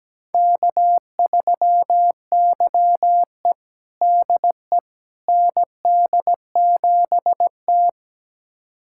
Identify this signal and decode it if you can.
Morse code, heard as a keyed tone, about 17 wpm, 700 Hz. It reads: K3YE DE ND7T
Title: USA Callsign DE Callsign - 4 Characters - 17wpm